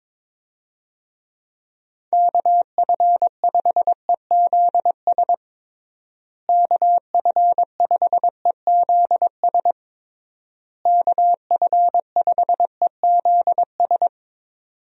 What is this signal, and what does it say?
Morse code, heard as a keyed tone, about 22 wpm, 700 Hz. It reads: KF5EZS KF5EZS KF5EZS
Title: USA Call Signs - 6 Characters - Repeated 3x - 22wpm